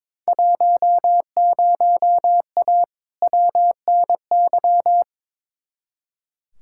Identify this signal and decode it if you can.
Morse code, heard as a keyed tone, about 21 wpm, 700 Hz. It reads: 10A WNY
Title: Field Day Exchanges - 22wpm